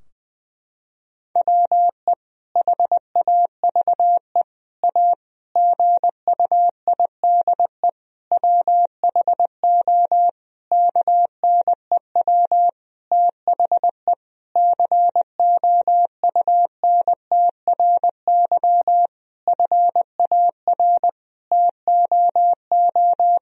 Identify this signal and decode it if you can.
Morse code, heard as a keyed tone, about 21 wpm, 700 Hz. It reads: WE HAVE A GUIDE WHO KNEW THE COUNTRY FAR TOO
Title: Sentences from Top 1000 Words - Review Part 3 - 20wpm